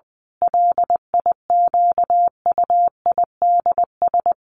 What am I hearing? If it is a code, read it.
Morse code, heard as a keyed tone, about 20 wpm, 700 Hz. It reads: LIQUIDS
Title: A Sound Of Thunder 20wpm